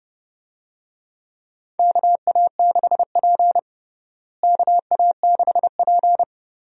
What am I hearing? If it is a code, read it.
Morse code, heard as a keyed tone, about 32 wpm, 700 Hz. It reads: KA6P KA6P
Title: USA Call Signs - 4 Characters - Repeated 2x - 30wpm